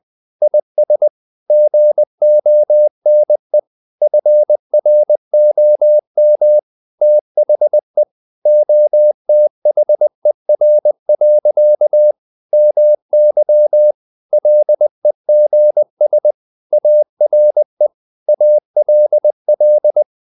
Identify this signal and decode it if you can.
Morse code, heard as a keyed tone, about 20 wpm, 600 Hz. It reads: IS GONE FROM THE OTHER. MY LEGS ARE ALL
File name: the old man and the sea part 2 20wpm